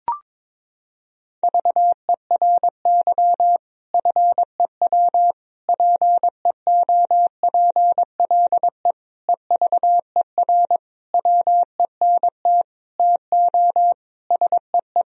Morse code, heard as a keyed tone, about 22 wpm, 700 Hz. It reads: VERY FEW PEOPLE EVER WENT TO SEE
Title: Sentences from Top 1000 Words - Review Part 1 - 22wpm